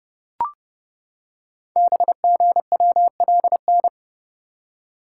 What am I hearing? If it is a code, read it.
Morse code, heard as a keyed tone, about 30 wpm, 700 Hz. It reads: BGWLN
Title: Code Groups - 5 Letters - 30wpm